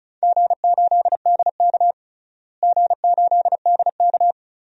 Morse code, heard as a keyed tone, about 36 wpm, 700 Hz. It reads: G8DK G8DK
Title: DXCC-100 Call Signs - 4 Characters - Repeated 3x - 35wpm